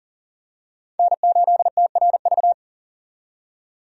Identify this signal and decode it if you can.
Morse code, heard as a keyed tone, about 40 wpm, 700 Hz. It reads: N8TRV
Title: USA Call Signs - 5 Characters - 40wpm